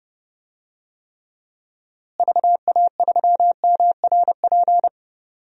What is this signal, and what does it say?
Morse code, heard as a keyed tone, about 31 wpm, 700 Hz. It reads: VA3MRP
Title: DXCC-100 Call Signs - 6 Characters - Repeated 2x - 30wpm